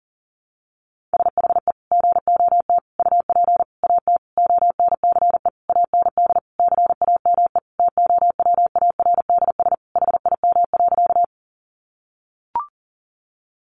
Morse code, heard as a keyed tone, about 40 wpm, 700 Hz. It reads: SHE GOT UP AT ONCE AND CAME TOWARDS HIM.
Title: Sentences from Top 1100 Words 40wpm